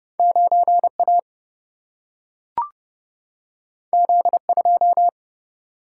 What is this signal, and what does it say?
Morse code, heard as a keyed tone, about 30 wpm, 700 Hz. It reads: 9 A Z 2
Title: Single-Letter Single-Number 30wpm